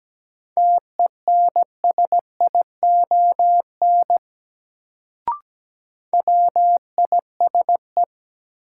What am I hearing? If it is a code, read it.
Morse code, heard as a keyed tone, about 17 wpm, 700 Hz. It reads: TENSION WISE
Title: Sentences from Top 1600 Words - New Word Drill Alt - 17wpm